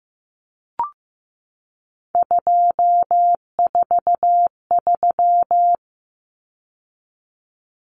Morse code, heard as a keyed tone, about 15 wpm, 700 Hz. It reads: 243
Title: Numbers - 3-Digit 15wpm